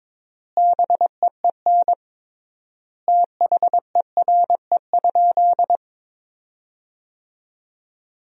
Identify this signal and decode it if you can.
Morse code, heard as a keyed tone, about 22 wpm, 700 Hz. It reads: BEEN THERE?
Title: 3x Word-Spacing - Sentences from Top 100 Words - 22wpm